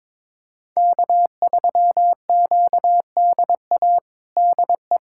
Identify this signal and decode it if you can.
Morse code, heard as a keyed tone, about 21 wpm, 700 Hz. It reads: K3QDA DE
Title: USA Callsign DE Callsign - 5 Characters - 22wpm